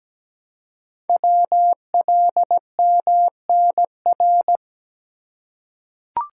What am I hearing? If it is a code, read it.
Morse code, heard as a keyed tone, about 17 wpm, 700 Hz. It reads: WLMNR
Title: Code Groups - 5 Letters - 17wpm